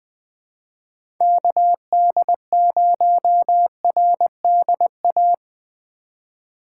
Morse code, heard as a keyed tone, about 21 wpm, 700 Hz. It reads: KD0RDA
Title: USA Call Signs - 6 Characters - 2x Character Spacing - 20wpm